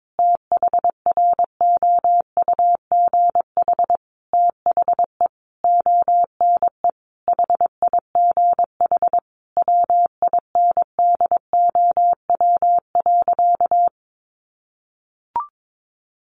Morse code, heard as a keyed tone, about 22 wpm, 700 Hz. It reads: THROUGH THE ONE HIGH WINDOW.
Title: A Sound Of Thunder 22wpm